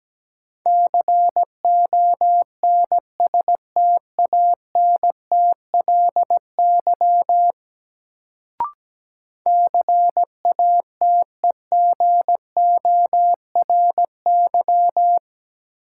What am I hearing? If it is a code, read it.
Morse code, heard as a keyed tone, about 17 wpm, 700 Hz. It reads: CONSTANTLY CATEGORY
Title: Sentences from Top 1000 Words - Review Part 7 - New Word Drill - 17wpm